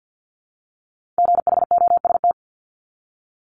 Morse code, heard as a keyed tone, about 50 wpm, 700 Hz. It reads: GHOST
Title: Multi-letter Phonemes Repeated 3x 50wpm